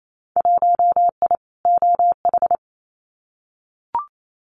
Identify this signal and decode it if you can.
Morse code, heard as a keyed tone, about 28 wpm, 700 Hz. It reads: 1I OH
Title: Winter Field Day Exchanges 28wpm